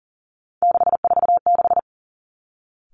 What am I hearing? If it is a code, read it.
Morse code, heard as a keyed tone, about 38 wpm, 700 Hz. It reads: 646
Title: Three Challenging Letter-Number Combinations 40wpm